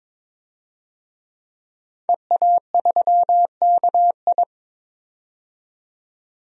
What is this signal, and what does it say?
Morse code, heard as a keyed tone, about 22 wpm, 700 Hz. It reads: EA3KI
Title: DXCC-100 Call Signs - 5 Characters - 22wpm